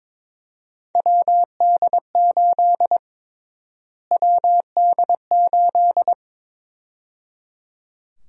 Morse code, heard as a keyed tone, about 22 wpm, 700 Hz. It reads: WD8 WD8
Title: USA Call Sign Prefixes - Repeated 2x - 22wpm